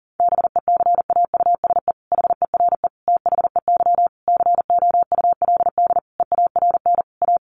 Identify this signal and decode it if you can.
Morse code, heard as a keyed tone, about 40 wpm, 700 Hz. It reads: BECAUSE HERE THEY COULD EARN A
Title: Sentences from Top 1500 Words 40wpm